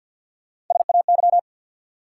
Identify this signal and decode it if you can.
Morse code, heard as a keyed tone, about 53 wpm, 700 Hz. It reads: SAY
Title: Sentences from Top 200 Words - New Word Drill - 50wpm